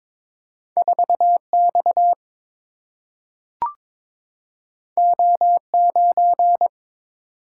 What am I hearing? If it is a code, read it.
Morse code, heard as a keyed tone, about 22 wpm, 700 Hz. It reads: 4 X O 9
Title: Single-Letter Single-Number 22wpm